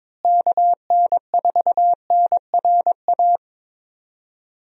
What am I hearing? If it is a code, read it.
Morse code, heard as a keyed tone, about 22 wpm, 700 Hz. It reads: KN4NRA
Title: USA Call Signs - 6 Characters - Repeated 3x - 22wpm